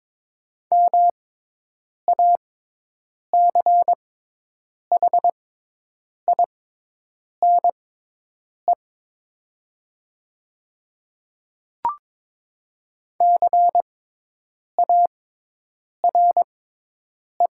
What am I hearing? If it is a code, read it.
Morse code, heard as a keyed tone, about 22 wpm, spaced as for 8 wpm, 700 Hz. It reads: MACHINE CARE
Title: 6x Character Spacing - Top 500 Words - 22wpm